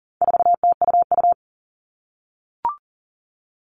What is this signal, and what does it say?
Morse code, heard as a keyed tone, about 40 wpm, 700 Hz. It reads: 4TUU
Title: All Cut Numbers - 4-Digit 40wpm